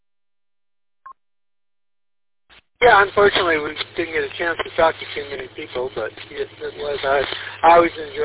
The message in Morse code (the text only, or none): none